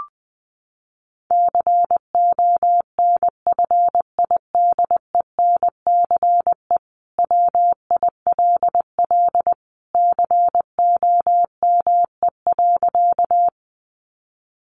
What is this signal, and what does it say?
Morse code, heard as a keyed tone, about 20 wpm, 700 Hz. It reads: CONFIDENCE WILL COME.
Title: Sentences from Top 1500 Words 20wpm